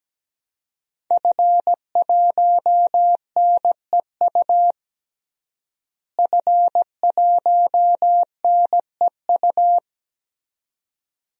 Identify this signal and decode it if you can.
Morse code, heard as a keyed tone, about 17 wpm, 700 Hz. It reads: F1NEU F1NEU
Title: DXCC-100 Call Signs - 5 Characters - Repeated 2x - 17wpm